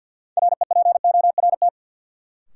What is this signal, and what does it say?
Morse code, heard as a keyed tone, about 48 wpm, 700 Hz. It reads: REPORT
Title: Sentences from Top 900 Words - New Word Drill - 50wpm